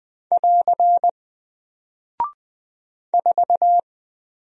Morse code, heard as a keyed tone, about 20 wpm, 700 Hz. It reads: <AR> 4
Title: Single Character (Letter:Number:Punctuation:Prosign) 20wpm